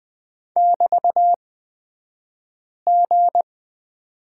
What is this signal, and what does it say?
Morse code, heard as a keyed tone, about 20 wpm, spaced as for 5 wpm, 700 Hz. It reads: <BT>G
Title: 242 Lesson 39d - Letter z - Two Characters 20wpm